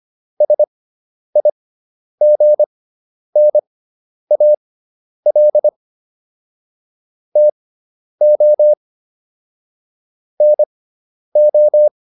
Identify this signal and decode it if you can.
Morse code, heard as a keyed tone, about 25 wpm, spaced as for 10 wpm, 600 Hz. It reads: SIGNAL TO NO